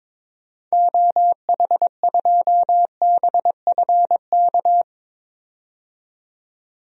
Morse code, heard as a keyed tone, about 22 wpm, 700 Hz. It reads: OH2BFK
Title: DXCC-100 Call Signs - 6 Characters - 22wpm